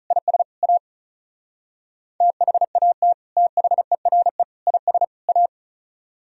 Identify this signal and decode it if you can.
Morse code, heard as a keyed tone, about 43 wpm, 700 Hz. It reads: IS A THAT THERE IS A
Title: Speed-Racing - Sets of 4 Words 35wpm